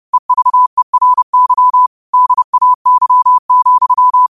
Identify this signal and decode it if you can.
Morse code, heard as a keyed tone, about 30 wpm, 1000 Hz. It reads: EVERY DAY,
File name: qotdfiles / 30wpm